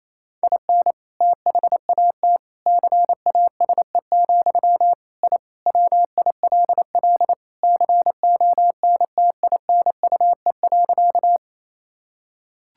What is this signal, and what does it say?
Morse code, heard as a keyed tone, about 28 wpm, 700 Hz. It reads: IN THAT CASE, I WILL CONTINUE.
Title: Sentences from Top 1000 Words - Review Part 6 - 28wpm